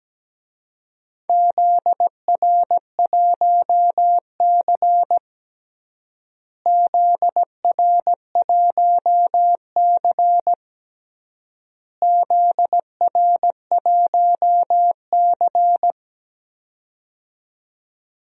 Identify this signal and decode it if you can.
Morse code, heard as a keyed tone, about 17 wpm, 700 Hz. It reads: ZR1C ZR1C ZR1C
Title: DXCC-100 Call Signs - 4 Characters - Repeated 3x - 17wpm